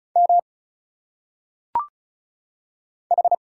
Morse code, heard as a keyed tone, about 36 wpm, 700 Hz. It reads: M H
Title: Single Letters - Phonetic Pronunciation 35wpm